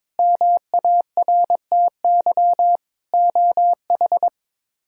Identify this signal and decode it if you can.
Morse code, heard as a keyed tone, about 21 wpm, 700 Hz. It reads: MARTY OH
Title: K1USN SST QSO Practice - 22wpm